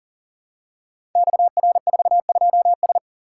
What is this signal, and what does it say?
Morse code, heard as a keyed tone, about 38 wpm, 700 Hz. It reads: XR42S
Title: Code Groups - 5 Letter:Numbers - 40wpm